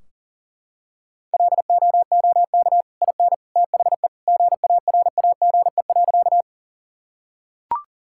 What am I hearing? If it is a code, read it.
Morse code, heard as a keyed tone, about 40 wpm, 700 Hz. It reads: LOOK IN THE GARAGE.
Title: Sentences from Top 1600 Words 40wpm